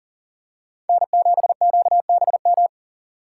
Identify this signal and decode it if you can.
Morse code, heard as a keyed tone, about 40 wpm, 700 Hz. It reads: N7QBM
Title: USA Call Signs - 5 Characters - 4x Character Spacing - 40wpm